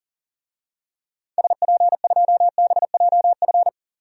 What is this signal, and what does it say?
Morse code, heard as a keyed tone, about 37 wpm, 700 Hz. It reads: SP2BJF
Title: DXCC-100 Call Signs - 6 Characters - 40wpm